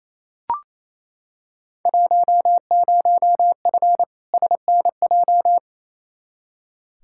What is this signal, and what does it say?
Morse code, heard as a keyed tone, about 28 wpm, 700 Hz. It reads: 10F SNJ
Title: Field Day Exchanges Encore - 28wpm